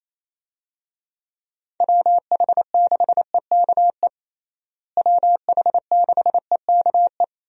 Morse code, heard as a keyed tone, about 27 wpm, 700 Hz. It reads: WH6EKE WH6EKE
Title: USA Call Signs - 6 Characters - Repeated 2x - 28wpm